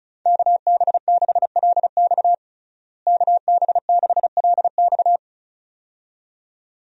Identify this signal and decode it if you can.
Morse code, heard as a keyed tone, about 35 wpm, 700 Hz. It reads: KB6LX KB6LX
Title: USA Call Signs - 5 Characters - Repeated 2x - 35wpm